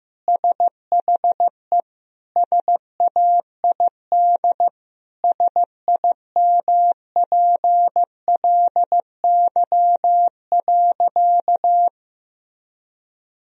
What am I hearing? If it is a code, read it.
Morse code, heard as a keyed tone, about 15 wpm, 700 Hz. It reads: SHE SAID SIMPLY.
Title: Sentences from Top 1000 Words - Review Part 4 - 15wpm